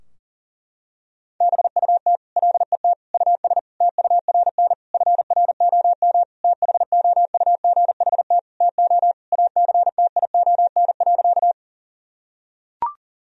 Morse code, heard as a keyed tone, about 40 wpm, 700 Hz. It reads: BUT LET US TURN FROM THOUGHT TO ACTION.
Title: Sentences from Top 1000 Words - Review Part 4 - 40wpm